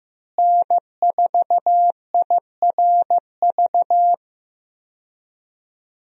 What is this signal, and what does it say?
Morse code, heard as a keyed tone, about 15 wpm, 700 Hz. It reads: N4IRV
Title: USA Call Signs - 5 Characters - Repeated 2x - 15wpm